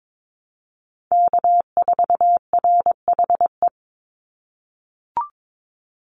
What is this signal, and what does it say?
Morse code, heard as a keyed tone, about 22 wpm, 700 Hz. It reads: K4RHE
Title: USA Call Signs 22wpm